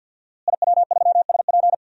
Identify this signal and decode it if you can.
Morse code, heard as a keyed tone, about 53 wpm, 700 Hz. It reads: IW3SP
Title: DXCC-100 Call Signs - 5 Characters - Repeated 2x - 50wpm